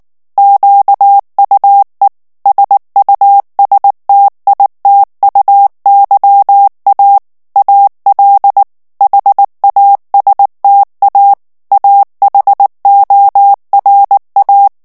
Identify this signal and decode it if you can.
Morse code, heard as a keyed tone, about 19 wpm, 800 Hz. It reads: QUE SUSTITUYA AL HASTA AHORA
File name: morse mp3 / ESPAGNOL TEXTE EN CLAIR 2 VITESSE 18 MOTS MINUTES